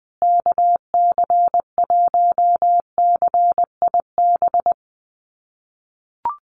Morse code, heard as a keyed tone, about 20 wpm, 700 Hz. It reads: KC1CIB
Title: USA Call Signs Repeated 3x 20wpm